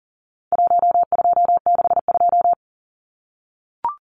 Morse code, heard as a keyed tone, about 40 wpm, 700 Hz. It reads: 1262
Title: Numbers - 4-Digit 40wpm